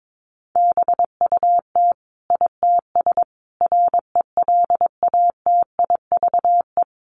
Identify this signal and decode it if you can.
Morse code, heard as a keyed tone, about 22 wpm, 700 Hz. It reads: BUT ITS RELATIVE
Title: Sentences from Top 1400 Words 22wpm